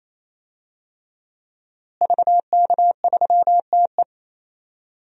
Morse code, heard as a keyed tone, about 28 wpm, 700 Hz. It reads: VK3TE
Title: DXCC-100 Call Signs - 5 Characters - Repeated 3x - 28wpm